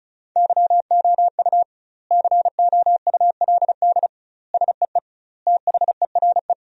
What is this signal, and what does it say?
Morse code, heard as a keyed tone, about 34 wpm, 700 Hz. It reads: YOU COULD SEE THERE
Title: 2x Word-Spacing - Sentences from Top 500 Words - 35wpm